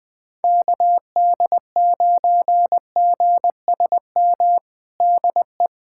Morse code, heard as a keyed tone, about 20 wpm, 700 Hz. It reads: KD9GSM DE
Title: USA Callsign DE Callsign 20wpm